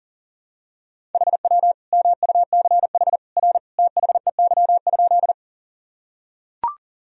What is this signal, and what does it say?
Morse code, heard as a keyed tone, about 39 wpm, 700 Hz. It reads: HW MUCH R THEY?
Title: Sentences from Top 200 Words and Abbreviations 40wpm